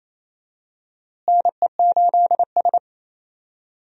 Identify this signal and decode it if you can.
Morse code, heard as a keyed tone, about 28 wpm, 700 Hz. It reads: NE8S
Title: USA Call Signs - 4 Characters - Repeated 3x - 28wpm